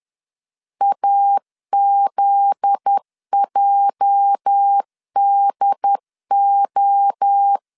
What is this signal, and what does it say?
Morse code, heard as a keyed tone, about 10 wpm, 800 Hz. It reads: AZJDO
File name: LEZIONI / 8 YJ45 / 8 YJ 10wpm 2651